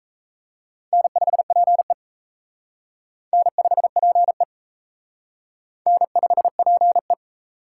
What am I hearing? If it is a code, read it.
Morse code, heard as a keyed tone, about 33 wpm, 700 Hz. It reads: N5PE N5PE N5PE
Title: Speed Racing - USA Call Signs 28wpm